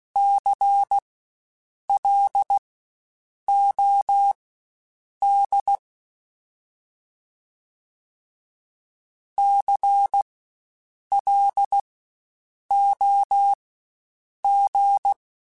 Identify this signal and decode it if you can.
Morse code, heard as a keyed tone, about 16 wpm, spaced as for 7 wpm, 800 Hz. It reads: CLOD CLOG
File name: Spaced Alphas / Spaced Alphas mp3s / Cs sp @15